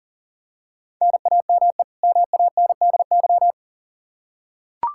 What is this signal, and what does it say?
Morse code, heard as a keyed tone, about 40 wpm, 700 Hz. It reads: NAME MANDY
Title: Practice QSO - Exchange 1 - Beginning - 40wpm